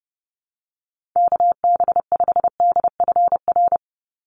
Morse code, heard as a keyed tone, about 30 wpm, 700 Hz. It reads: KB5DFR
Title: USA Call Signs 30wpm